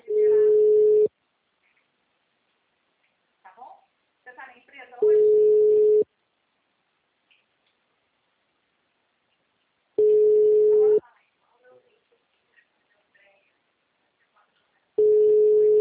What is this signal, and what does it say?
Ringback tone (Europe, 425 Hz), with speech.